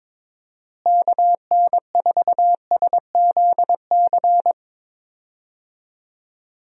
Morse code, heard as a keyed tone, about 22 wpm, 700 Hz. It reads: KN4SZC
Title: USA Call Signs - 6 Characters - Repeated 2x - 22wpm